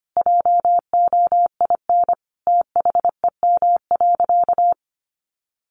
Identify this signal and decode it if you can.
Morse code, heard as a keyed tone, about 25 wpm, 700 Hz. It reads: JOIN THEM.
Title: Sentences from Top 1400 Words 25wpm